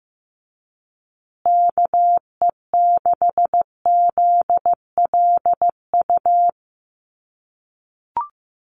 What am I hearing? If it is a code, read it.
Morse code, heard as a keyed tone, about 15 wpm, 700 Hz. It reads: KE6ZLU
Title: USA Call Signs Repeated 2x 15wpm